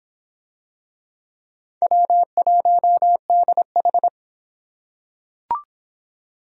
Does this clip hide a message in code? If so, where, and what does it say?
Yes: Morse code from 1.8 to 4.1 s, heard as a keyed tone, about 25 wpm, 700 Hz. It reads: W1DH